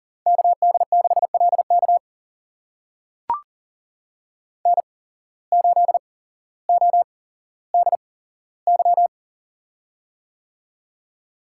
Morse code, heard as a keyed tone, about 39 wpm, spaced as for 17 wpm, 700 Hz. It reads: KD6LK N8ODY
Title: USA Call Signs - 5 Characters - 8x Character Spacing - 40wpm